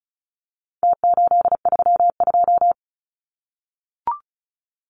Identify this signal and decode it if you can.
Morse code, heard as a keyed tone, about 34 wpm, 700 Hz. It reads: T832
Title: Numbers - Contest Style - 4-Digit 35wpm